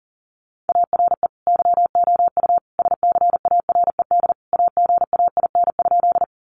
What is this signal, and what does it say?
Morse code, heard as a keyed tone, about 40 wpm, 700 Hz. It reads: ARE YOU SCARED AGAIN?
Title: Sentences from Top 700 Words 40wpm